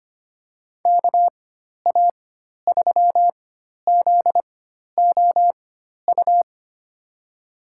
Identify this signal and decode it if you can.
Morse code, heard as a keyed tone, about 25 wpm, spaced as for 12 wpm, 700 Hz. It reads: KA3ZOU
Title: USA Call Signs - 6 Characters - 4x Character Spacing - 25wpm